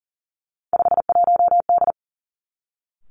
Morse code, heard as a keyed tone, about 40 wpm, 700 Hz. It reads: 51D
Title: Three Challenging Letter-Number Combinations 40wpm